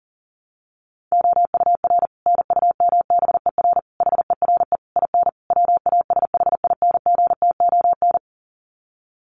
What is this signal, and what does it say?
Morse code, heard as a keyed tone, about 40 wpm, 700 Hz. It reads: OUR NUMBER HERE IN WASHINGTON
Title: Sets of 5 Words Encore2 40wpm